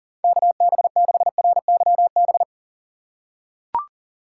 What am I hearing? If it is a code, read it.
Morse code, heard as a keyed tone, about 40 wpm, 700 Hz. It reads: KB6RYB
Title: USA Call Signs - 6 Characters - 3x Character Spacing - 40wpm